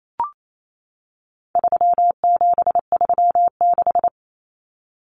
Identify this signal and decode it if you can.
Morse code, heard as a keyed tone, about 28 wpm, 700 Hz. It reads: 3736
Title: Numbers - 4-Digit 28wpm